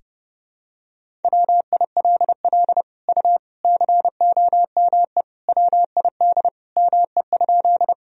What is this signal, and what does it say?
Morse code, heard as a keyed tone, about 30 wpm, 700 Hz. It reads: WILL U COME WID ME?
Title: Sentences from Top 200 Words and Abbreviations 30wpm